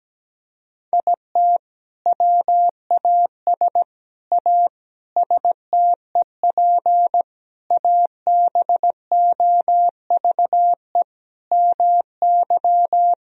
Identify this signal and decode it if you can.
Morse code, heard as a keyed tone, about 17 wpm, 700 Hz. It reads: IT WAS A STEP ABOVE MY
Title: Sentences from Top 1000 Words - Review Part 3 - 17wpm